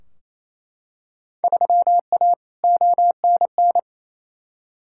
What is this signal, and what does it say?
Morse code, heard as a keyed tone, about 28 wpm, 700 Hz. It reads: 3A ONN
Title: Field Day Exchanges Encore - 28wpm